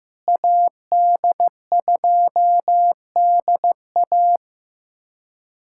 Morse code, heard as a keyed tone, about 15 wpm, 700 Hz. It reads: AD2DA
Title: USA Callsign DE Callsign - 5 Characters - 15wpm